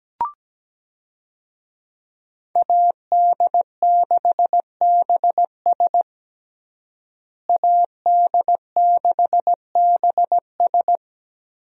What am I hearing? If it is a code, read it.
Morse code, heard as a keyed tone, about 17 wpm, 700 Hz. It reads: AD6BS AD6BS
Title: USA Call Signs - 5 Characters - Repeated 3x - 17wpm